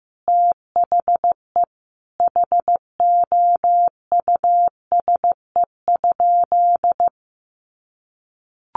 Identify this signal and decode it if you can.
Morse code, heard as a keyed tone, about 15 wpm, 700 Hz. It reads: THE HOUSE?
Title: Sentences from Top 800 Words 15wpm